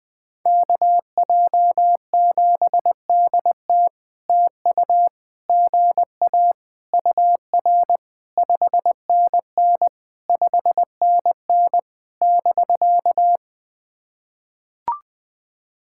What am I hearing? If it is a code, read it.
Morse code, heard as a keyed tone, about 18 wpm, 700 Hz. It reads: KJ7DT TU GA UR 5NN 5NN <BK>
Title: POTA QSO Practice - 20wpm